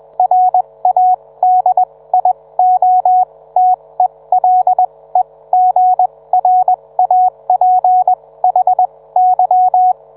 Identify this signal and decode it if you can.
Morse code, heard as a keyed tone, about 21 wpm, spaced as for 17 wpm, 740 Hz. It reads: RADIOTELEGRAPHY